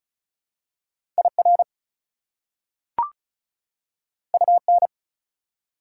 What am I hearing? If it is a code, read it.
Morse code, heard as a keyed tone, about 34 wpm, 700 Hz. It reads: IR UN
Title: Prefixes and Suffixes 35wpm